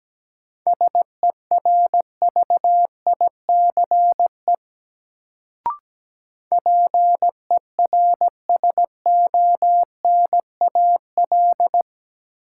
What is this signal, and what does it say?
Morse code, heard as a keyed tone, about 17 wpm, 700 Hz. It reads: SERVICE PERSONAL
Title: Sentences from Top 700 Words - New Word Drill - 17wpm